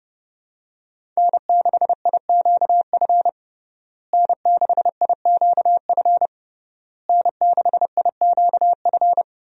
Morse code, heard as a keyed tone, about 30 wpm, 700 Hz. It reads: N6IQF N6IQF N6IQF
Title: USA Call Signs - 5 Characters - Repeated 3x - 30wpm